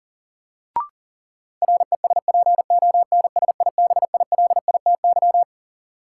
Morse code, heard as a keyed tone, about 40 wpm, 700 Hz. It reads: RESPONSIBILITY
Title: Sentences from Top 1200 Words - New Word Drill - 40wpm